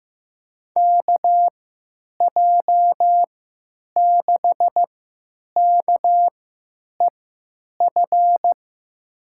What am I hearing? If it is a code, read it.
Morse code, heard as a keyed tone, about 15 wpm, spaced as for 9 wpm, 700 Hz. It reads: KJ6KEF
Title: USA Call Signs - 6 Characters - 3x Character Spacing - 15wpm